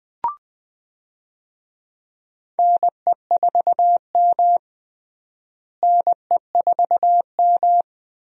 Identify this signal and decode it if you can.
Morse code, heard as a keyed tone, about 20 wpm, 700 Hz. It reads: NE4M NE4M
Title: USA Call Signs - 4 Characters - Repeated 3x - 20wpm